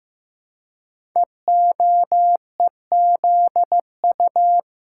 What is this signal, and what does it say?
Morse code, heard as a keyed tone, about 15 wpm, 700 Hz. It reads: EOEZU
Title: Code Groups - 5 Letters - 15wpm